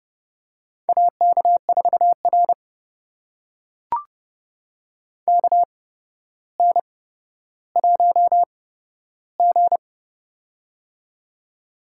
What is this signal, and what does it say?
Morse code, heard as a keyed tone, about 30 wpm, spaced as for 13 wpm, 700 Hz. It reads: AK4R KN1G
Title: USA Call Signs - 4 Characters - 8x Character Spacing - 30wpm